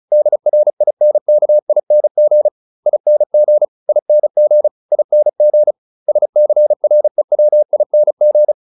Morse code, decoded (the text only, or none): DRINKING ING ING ING SCREWING